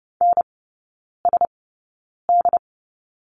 Morse code, heard as a keyed tone, about 30 wpm, spaced as for 9 wpm, 700 Hz. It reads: NSD
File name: Lesson 19e - Letter w - Three Characters 30wpm